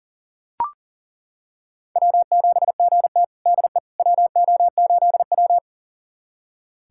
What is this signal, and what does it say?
Morse code, heard as a keyed tone, about 40 wpm, 700 Hz. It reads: W7GT DE WO8W
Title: USA Callsign DE Callsign - 4 Characters - 40wpm